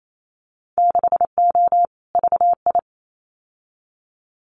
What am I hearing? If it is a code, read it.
Morse code, heard as a keyed tone, about 28 wpm, 690 Hz. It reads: 6O VI
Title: Winter Field Day Exchanges Encore 28wpm